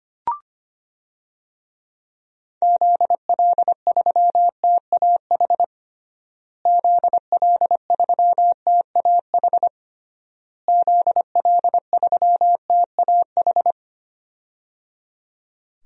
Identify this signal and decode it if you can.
Morse code, heard as a keyed tone, about 25 wpm, 700 Hz. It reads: ZL3TAH ZL3TAH ZL3TAH
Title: DXCC-100 Call Signs - 6 Characters - Repeated 3x - 25wpm